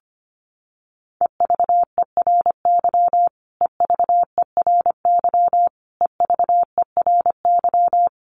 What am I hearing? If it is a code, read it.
Morse code, heard as a keyed tone, about 25 wpm, 700 Hz. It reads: EVERY EVERY EVERY
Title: Top 500 Words Repeated 25wpm